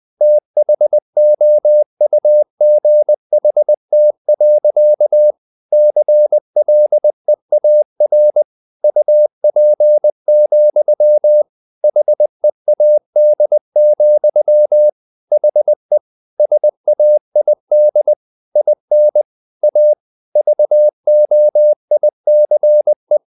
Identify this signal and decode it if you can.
Morse code, heard as a keyed tone, about 20 wpm, 600 Hz. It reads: THOUGHT. CLEAR UP, HEAD, HE SAID IN A VOICE
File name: the old man and the sea part 2 20wpm